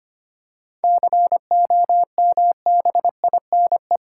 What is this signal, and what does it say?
Morse code, heard as a keyed tone, about 25 wpm, 700 Hz. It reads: COMBINE
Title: Sentences from Top 1600 Words 25wpm